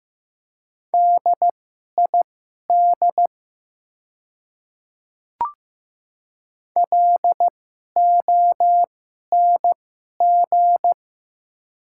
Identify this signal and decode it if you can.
Morse code, heard as a keyed tone, about 15 wpm, spaced as for 11 wpm, 700 Hz. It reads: DID LONG
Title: 2x Character Spacing - Top 200 Words - 15wpm